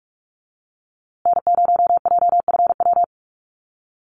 Morse code, heard as a keyed tone, about 44 wpm, 700 Hz. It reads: N0JFW
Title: USA Call Signs Repeated 2x 45wpm